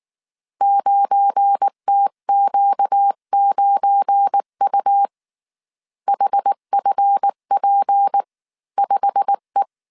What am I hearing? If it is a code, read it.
Morse code, heard as a keyed tone, about 19 wpm, 800 Hz. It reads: 9TQ9U HFP5E
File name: CW / DET / 1 9 1500C S B 18wpm 2551